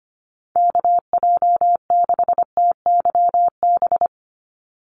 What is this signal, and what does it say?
Morse code, heard as a keyed tone, about 25 wpm, 700 Hz. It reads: KJ6TYB
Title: USA Call Signs 25wpm